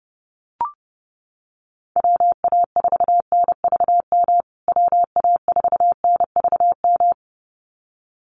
Morse code, heard as a keyed tone, about 28 wpm, 700 Hz. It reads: WA4NVM WA4NVM
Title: USA Call Signs Repeated 2x 30wpm